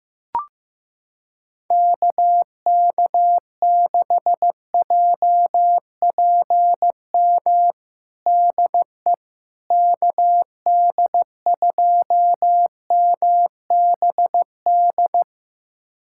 Morse code, heard as a keyed tone, about 16 wpm, 700 Hz. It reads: KK6JPM DE KD2MBD
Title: USA Callsign DE Callsign - 6 Characters - 15wpm